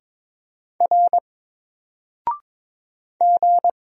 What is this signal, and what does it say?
Morse code, heard as a keyed tone, about 22 wpm, 700 Hz. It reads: R G
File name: Single-Letter-Number-22wpm